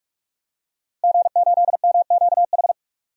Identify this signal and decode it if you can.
Morse code, heard as a keyed tone, about 42 wpm, 700 Hz. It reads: G8MQH
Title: DXCC-100 Call Signs - 5 Characters - Repeated 3x - 45wpm